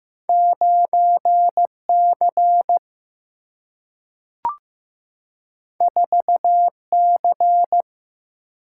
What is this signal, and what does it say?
Morse code, heard as a keyed tone, about 15 wpm, 700 Hz. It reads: 9 C 4 C
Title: Single-Letter Single-Number 15wpm